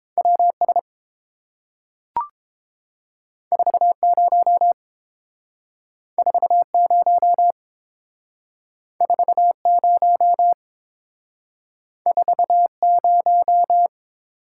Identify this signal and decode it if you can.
Morse code, heard as a keyed tone, about 30 wpm, 700 Hz. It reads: WS 40 40 40 40